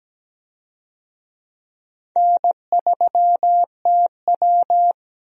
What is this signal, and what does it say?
Morse code, heard as a keyed tone, about 17 wpm, 700 Hz. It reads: N3TW
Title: USA Call Signs - 4 Characters - Repeated 3x - 17wpm